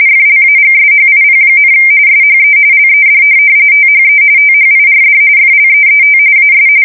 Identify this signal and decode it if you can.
RTTY, about 45.45 baud. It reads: FLAGSHIP, SHE LED THE THIRD FLEET AS
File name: fsk170 / USS-New-Jersey